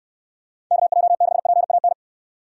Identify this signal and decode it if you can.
Morse code, heard as a keyed tone, about 69 wpm, 700 Hz. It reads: KJ6PAN